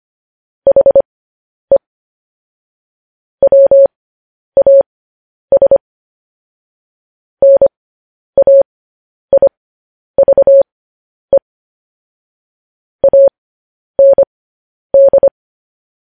Morse code, heard as a keyed tone, about 25 wpm, spaced as for 10 wpm, 550 Hz. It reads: HE WAS NAIVE AND
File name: podcast / News headlines-1x-25x10